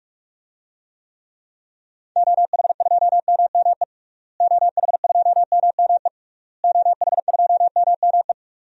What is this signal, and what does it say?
Morse code, heard as a keyed tone, about 46 wpm, 700 Hz. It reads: OH2MME OH2MME OH2MME
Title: DXCC-100 Call Signs - 6 Characters - Repeated 3x - 45wpm